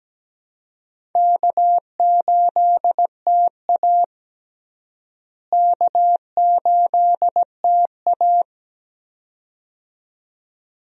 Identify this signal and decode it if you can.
Morse code, heard as a keyed tone, about 17 wpm, 700 Hz. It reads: K8TA K8TA
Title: USA Call Signs - 4 Characters - Repeated 2x - 17wpm